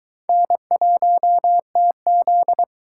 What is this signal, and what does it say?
Morse code, heard as a keyed tone, about 22 wpm, 700 Hz. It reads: N1TZ